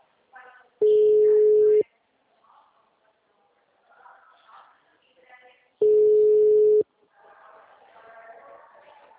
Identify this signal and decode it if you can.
Ringback tone (Europe, 425 Hz), with speech.